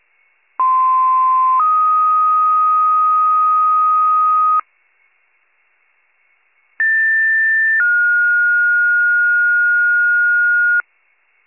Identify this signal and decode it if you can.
Two-tone page, heard as a one-second tone, then a three-second tone, about 1040 then 1250 Hz, about 1800 then 1430 Hz. It is heard from 0.6 to 10.8 s.